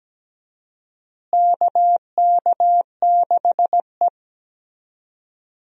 Morse code, heard as a keyed tone, about 17 wpm, 700 Hz. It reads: KK6E
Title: USA Call Signs - 4 Characters - Repeated 3x - 17wpm